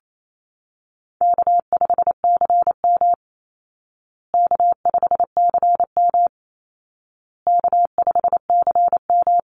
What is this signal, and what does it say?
Morse code, heard as a keyed tone, about 28 wpm, 700 Hz. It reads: K5CM K5CM K5CM
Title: Call Signs Repeated 2x - 28wpm